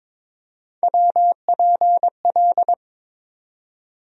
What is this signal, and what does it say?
Morse code, heard as a keyed tone, about 22 wpm, 700 Hz. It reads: WPL
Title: USA Call Sign Postfixes - 3 Letters 22wpm